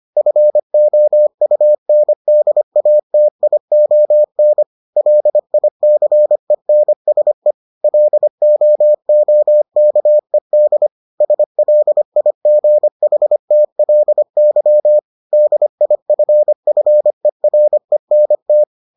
Morse code, heard as a keyed tone, about 25 wpm, 600 Hz. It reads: FOUNDATION LICENSE LOOKED SLIGHTLY DIFFERENT